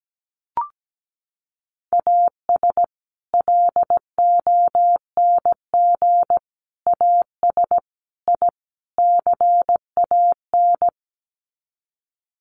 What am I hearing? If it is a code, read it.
Morse code, heard as a keyed tone, about 17 wpm, 700 Hz. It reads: AS LONG AS I CAN
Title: Sets of 5 Words Encore2 17wpm